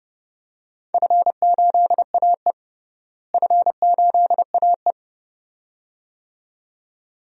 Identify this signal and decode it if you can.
Morse code, heard as a keyed tone, about 29 wpm, 700 Hz. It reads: F8AE F8AE
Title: DXCC-100 Call Signs - 4 Characters - Repeated 2x - 30wpm